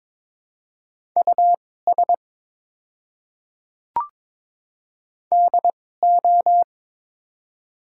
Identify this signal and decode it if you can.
Morse code, heard as a keyed tone, about 23 wpm, spaced as for 16 wpm, 700 Hz. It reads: US DO
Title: 2x Character Spacing - Top 300 Words - 22wpm